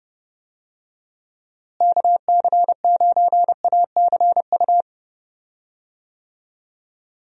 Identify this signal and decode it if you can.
Morse code, heard as a keyed tone, about 30 wpm, 700 Hz. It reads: KC9ACU
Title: USA Call Signs - 6 Characters - Repeated 3x - 30wpm